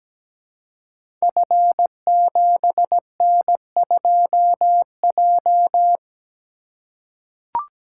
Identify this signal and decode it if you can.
Morse code, heard as a keyed tone, about 17 wpm, 700 Hz. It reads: F7N2J
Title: Code Groups - 5 Letter:Numbers - 17wpm